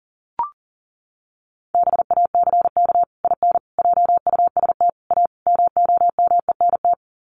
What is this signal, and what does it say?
Morse code, heard as a keyed tone, about 40 wpm, 700 Hz. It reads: BACK IN JUST A MOMENT
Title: Sets of 5 Words Encore 40wpm